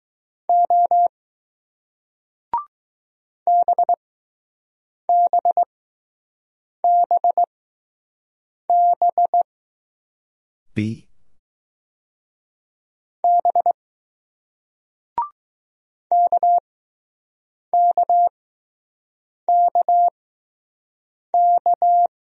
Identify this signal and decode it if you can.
Morse code, heard as a keyed tone, about 20 wpm, spaced as for 6 wpm, 700 Hz. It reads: O BBBB B KKKK